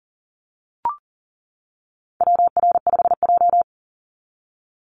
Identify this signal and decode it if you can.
Morse code, heard as a keyed tone, about 38 wpm, 700 Hz. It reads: WR5J
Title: Call Signs - 40wpm